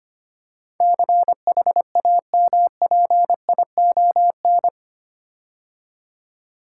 Morse code, heard as a keyed tone, about 25 wpm, 700 Hz. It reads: CHAMPION
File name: 8 Letter Words - First 3 Letters Given - 25wpm